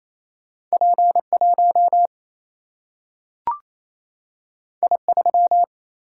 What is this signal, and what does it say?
Morse code, heard as a keyed tone, about 28 wpm, 700 Hz. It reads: P1 I3
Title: Single-Letter Single-Number 28wpm